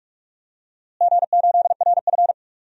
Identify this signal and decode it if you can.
Morse code, heard as a keyed tone, about 46 wpm, 700 Hz. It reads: G8RF
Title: DXCC-100 Call Signs - 4 Characters - Repeated 3x - 45wpm